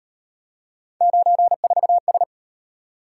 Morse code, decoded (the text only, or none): 94S